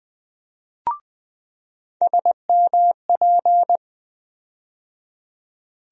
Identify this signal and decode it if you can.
Morse code, heard as a keyed tone, about 20 wpm, 700 Hz. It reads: SMP
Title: USA Call Sign Postfixes - 3 Letters 20wpm